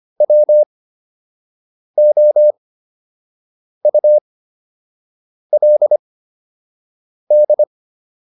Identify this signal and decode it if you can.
Morse code, heard as a keyed tone, about 25 wpm, spaced as for 6 wpm, 600 Hz. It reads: WOULD